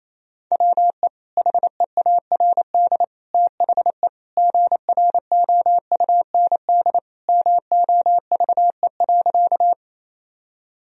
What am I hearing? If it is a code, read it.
Morse code, heard as a keyed tone, about 28 wpm, 700 Hz. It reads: WE HEARD THE GROUND MOVE.
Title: Sentences from Top 1000 Words - Review Part 2 - 28wpm